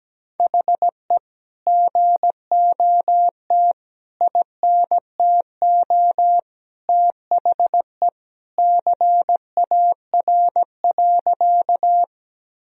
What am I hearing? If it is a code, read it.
Morse code, heard as a keyed tone, about 17 wpm, 700 Hz. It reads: HE GOT INTO THE CAR.
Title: Sentences from Top 1000 Words - Review Part 1 - 17wpm